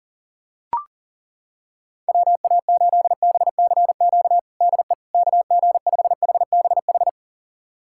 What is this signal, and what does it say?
Morse code, heard as a keyed tone, about 40 wpm, 700 Hz. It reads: WA8BCQ DE KG5HBH
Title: USA Callsign DE Callsign - 6 Characters - 40wpm